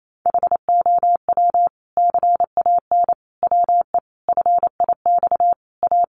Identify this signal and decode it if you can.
Morse code, heard as a keyed tone, about 28 wpm, 700 Hz. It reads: HOW CAN WE FIX A
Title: Sentences from Top 1400 Words 28wpm